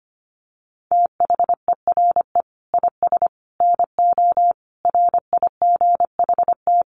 Morse code, heard as a keyed tone, about 25 wpm, 700 Hz. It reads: THERE IS NO RIGHT
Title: Sentences from Top 1500 Words 25wpm